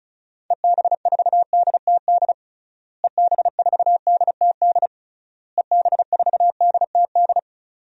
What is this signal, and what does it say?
Morse code, heard as a keyed tone, about 35 wpm, 700 Hz. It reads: EB4DTD EB4DTD EB4DTD
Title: DXCC-100 Call Signs - 6 Characters - Repeated 3x - 35wpm